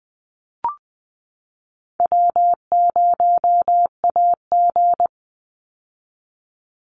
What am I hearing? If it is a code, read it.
Morse code, heard as a keyed tone, about 20 wpm, 700 Hz. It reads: W0AG
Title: Easy Call Signs - 20wpm